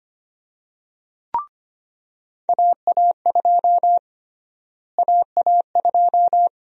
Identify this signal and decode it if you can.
Morse code, heard as a keyed tone, about 25 wpm, 700 Hz. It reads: AA2 AA2
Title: USA Call Sign Prefixes - Repeated 2x - 25wpm